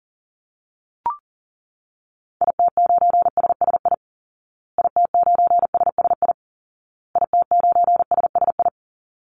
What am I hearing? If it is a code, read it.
Morse code, heard as a keyed tone, about 40 wpm, 700 Hz. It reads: IT9SSI IT9SSI IT9SSI
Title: Call Signs Repeated 2x - 40wpm